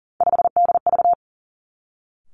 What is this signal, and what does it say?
Morse code, heard as a keyed tone, about 38 wpm, 700 Hz. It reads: 5DV
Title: Three Challenging Letter-Number Combinations 40wpm